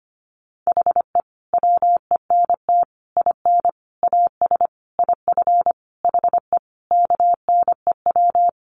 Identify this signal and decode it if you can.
Morse code, heard as a keyed tone, about 25 wpm, 700 Hz. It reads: HE WENT IN AS IF HE KNEW
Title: Sentences from Top 1400 Words 25wpm